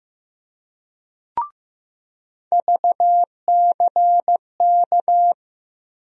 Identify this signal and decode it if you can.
Morse code, heard as a keyed tone, about 15 wpm, 700 Hz. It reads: VCK
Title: USA Call Sign Postfixes - 3 Letters 15wpm